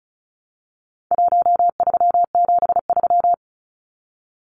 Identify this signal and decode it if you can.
Morse code, heard as a keyed tone, about 34 wpm, 700 Hz. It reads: 1373
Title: Numbers - Contest Style - 4-Digit 35wpm